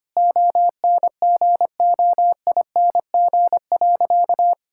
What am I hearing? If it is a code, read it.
Morse code, heard as a keyed tone, about 25 wpm, 700 Hz. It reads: ONGOING.
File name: practice / 25wpm / Sentences with 3-Word Phrases - 25wpm